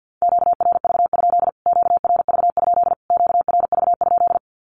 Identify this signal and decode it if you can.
Morse code, heard as a keyed tone, about 48 wpm, 700 Hz. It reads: QRV? QRV? QRV?
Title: Q-Codes Repeated 3x 50wpm